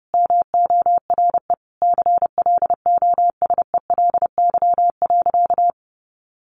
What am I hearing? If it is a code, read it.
Morse code, heard as a keyed tone, about 30 wpm, 700 Hz. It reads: MORE CLOSELY.
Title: Sentences from Top 1400 Words 30wpm